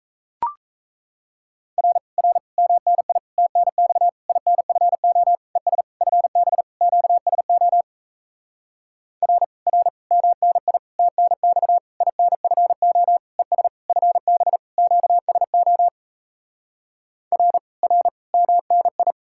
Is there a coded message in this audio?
Yes: Morse code from 1.8 to 19.1 s, heard as a keyed tone, about 37 wpm, 700 Hz. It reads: R R MNI TNX INFO ES FB QSO R R MNI TNX INFO ES FB QSO R R MNI